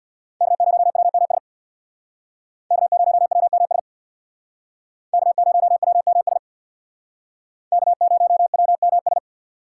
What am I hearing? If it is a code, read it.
Morse code, heard as a keyed tone, about 63 wpm, 700 Hz. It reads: K0WMS K0WMS K0WMS K0WMS